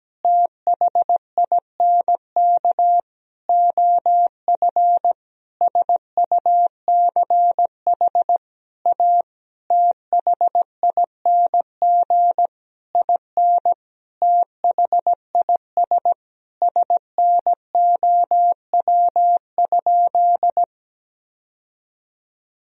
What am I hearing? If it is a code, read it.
Morse code, heard as a keyed tone, about 17 wpm, 700 Hz. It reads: THINK OF SUCH A THING IN THIS SNOW?
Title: Sentences from Top 1000 Words - Review Part 3 - 17wpm